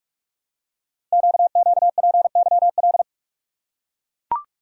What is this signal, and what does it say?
Morse code, heard as a keyed tone, about 44 wpm, 700 Hz. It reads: QQPYL
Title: Code Groups - 5 Letters - 45wpm